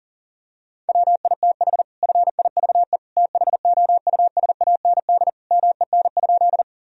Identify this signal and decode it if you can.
Morse code, heard as a keyed tone, about 40 wpm, 700 Hz. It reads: WITH FIVE THOUSAND MEN?
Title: Sentences from Top 1000 Words - Review Part 3 - 40wpm